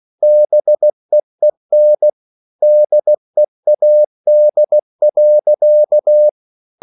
Morse code, heard as a keyed tone, about 16 wpm, 600 Hz. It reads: BEEN DEAD.